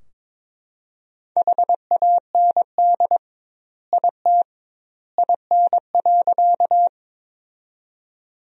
Morse code, heard as a keyed tone, about 22 wpm, 700 Hz. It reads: HAND IT IN.
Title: 2x Word-Spacing - Sentences from Top 200 Words - 22wpm